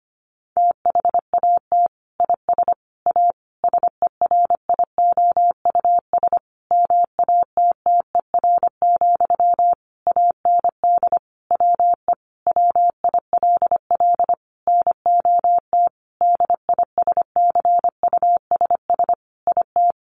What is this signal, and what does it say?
Morse code, heard as a keyed tone, about 25 wpm, 700 Hz. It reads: THAT IS A SERIOUS MATTER, AND WE WILL NOT DISCUSS IT
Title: Sentences from Top 1100 Words 25wpm